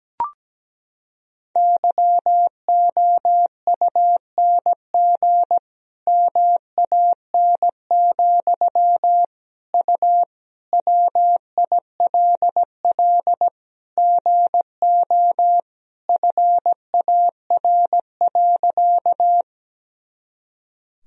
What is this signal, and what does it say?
Morse code, heard as a keyed tone, about 17 wpm, 700 Hz. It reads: YOUNG MAN, U WILL GO FAR.
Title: Sentences from Top 400 Words and Abbreviations 17wpm